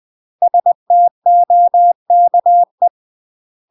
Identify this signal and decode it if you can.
Morse code, heard as a keyed tone, about 20 wpm, 700 Hz. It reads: STOKE
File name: UK MASTER QSOs TRAINING / 20WPM / UK MASTER QSOs 0048